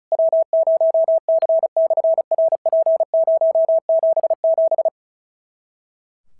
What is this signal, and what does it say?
Morse code, heard as a keyed tone, about 36 wpm, 650 Hz. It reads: W0C/RP077
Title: SOTA QSO Practice 35wpm